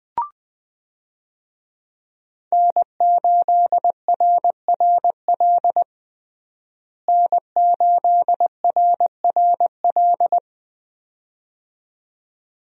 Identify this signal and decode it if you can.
Morse code, heard as a keyed tone, about 20 wpm, 700 Hz. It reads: N8RRL N8RRL
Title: USA Call Signs - 5 Characters - Repeated 2x - 20wpm